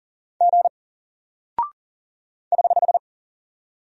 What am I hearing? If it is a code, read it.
Morse code, heard as a keyed tone, about 40 wpm, 700 Hz. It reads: G <HH>
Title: Single Character (Letter:Number:Punctuation:Prosign) Alt 40wpm